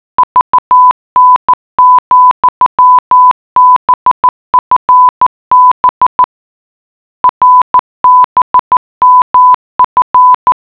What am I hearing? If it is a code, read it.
Morse code, heard as a keyed tone, about 14 wpm, 1000 Hz. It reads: VN, BFB RBMF